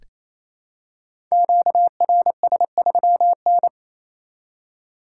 Morse code, heard as a keyed tone, about 27 wpm, 700 Hz. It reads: QRS3N
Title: Code Groups - 5 Letter:Numbers - 28wpm